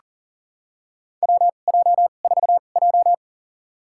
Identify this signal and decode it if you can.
Morse code, heard as a keyed tone, about 42 wpm, spaced as for 29 wpm, 690 Hz. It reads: WJ4J
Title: USA Call Signs - 4 Characters - 2x Character Spacing - 40wpm